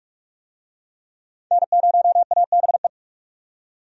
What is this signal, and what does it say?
Morse code, heard as a keyed tone, about 43 wpm, 700 Hz. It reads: N0ABE
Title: USA Call Signs - 5 Characters - 45wpm